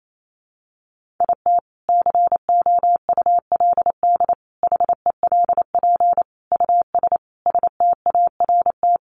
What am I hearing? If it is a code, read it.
Morse code, heard as a keyed tone, about 28 wpm, 700 Hz. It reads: IT COULD HELP US START
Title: Sentences from Top 1400 Words 28wpm